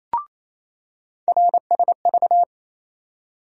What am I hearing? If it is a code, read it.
Morse code, heard as a keyed tone, about 28 wpm, 700 Hz. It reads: RSV
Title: USA Call Sign Postfixes - 3 Letters 28wpm